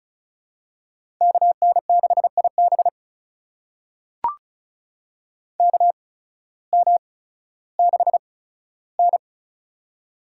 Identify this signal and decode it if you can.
Morse code, heard as a keyed tone, about 34 wpm, spaced as for 16 wpm, 700 Hz. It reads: KN6IB KM6N
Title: USA Call Signs - 8x Character Spacing - 35wpm